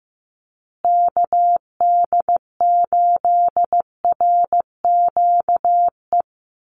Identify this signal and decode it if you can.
Morse code, heard as a keyed tone, about 15 wpm, 700 Hz. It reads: KD8RQE
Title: Call Signs Repeated 2x - 15wpm